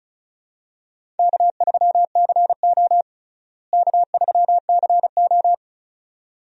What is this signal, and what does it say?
Morse code, heard as a keyed tone, about 36 wpm, 700 Hz. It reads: K3CO K3CO
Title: USA Call Signs - 4 Characters - Repeated 2x - 35wpm